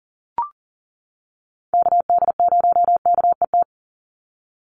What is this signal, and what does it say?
Morse code, heard as a keyed tone, about 40 wpm, 700 Hz. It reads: KD0KET
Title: USA Call Signs 40wpm